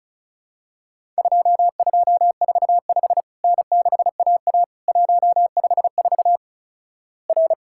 Morse code, heard as a keyed tone, about 34 wpm, 700 Hz. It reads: 2245 N6AA 154 R
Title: Worked All Europe DX Contest - QTC Exchange - 35wpm